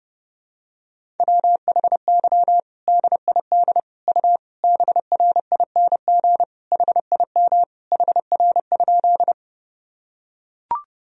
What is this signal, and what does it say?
Morse code, heard as a keyed tone, about 30 wpm, 700 Hz. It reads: WHY DID U BRING HIM HR?
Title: Sentences from Top 500 Words and Abbreviations 30wpm